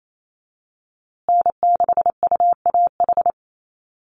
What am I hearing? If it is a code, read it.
Morse code, heard as a keyed tone, about 28 wpm, 700 Hz. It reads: N6UAH
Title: USA Call Signs Repeated 2x 28wpm